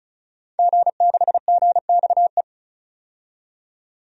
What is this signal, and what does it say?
Morse code, heard as a keyed tone, about 36 wpm, 700 Hz. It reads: G6GXE
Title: DXCC-100 Call Signs - 5 Characters - 35wpm